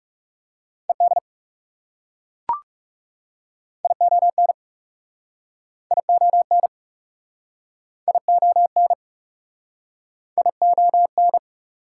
Morse code, heard as a keyed tone, about 40 wpm, 690 Hz. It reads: ED ION ION ION ION